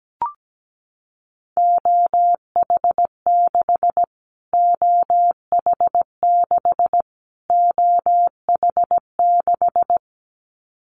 Morse code, heard as a keyed tone, about 17 wpm, 700 Hz. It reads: OH6 OH6 OH6
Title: Callsign Prefixes for Top 25 DXCC - Repeated 3x - 17wpm